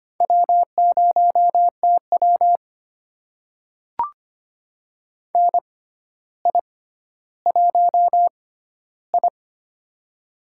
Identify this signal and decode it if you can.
Morse code, heard as a keyed tone, about 25 wpm, spaced as for 13 wpm, 700 Hz. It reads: W0TW NI1I
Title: USA Call Signs - 4 Characters - 6x Character Spacing - 25wpm